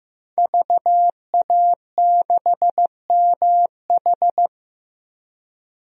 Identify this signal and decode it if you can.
Morse code, heard as a keyed tone, about 15 wpm, 700 Hz. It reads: VA6MH
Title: DXCC-100 Call Signs - 5 Characters - Repeated 3x - 15wpm